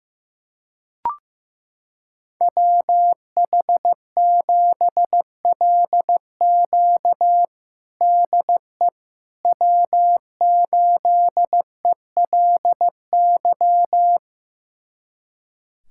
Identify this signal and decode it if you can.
Morse code, heard as a keyed tone, about 15 wpm, 700 Hz. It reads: WH7LQ DE W8ELY
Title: USA Callsign DE Callsign - 5 Characters - 15wpm